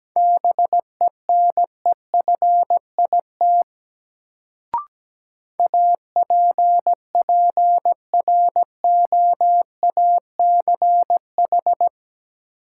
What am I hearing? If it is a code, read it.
Morse code, heard as a keyed tone, about 17 wpm, 700 Hz. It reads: BENEFIT APPROACH
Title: Sentences from Top 1000 Words - Review Part 6 - New Word Drill - 17wpm